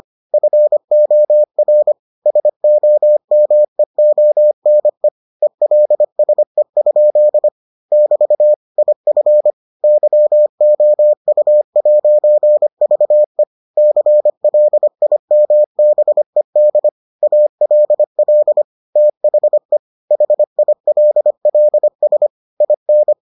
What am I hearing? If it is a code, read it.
Morse code, heard as a keyed tone, about 25 wpm, 600 Hz. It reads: FOR SOMEONE ELSE? <BT> IF YOU'VE CLIMBED ALL THE HILLS IN